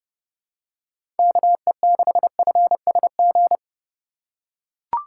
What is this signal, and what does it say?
Morse code, heard as a keyed tone, about 30 wpm, 700 Hz. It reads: KE6FSG
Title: USA Call Signs - 6 Characters - 2x Character Spacing - 30wpm